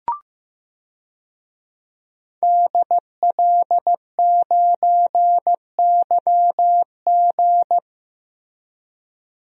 Morse code, heard as a keyed tone, about 15 wpm, 700 Hz. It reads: DL9YG
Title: DXCC-100 Call Signs - 5 Characters - Repeated 2x - 15wpm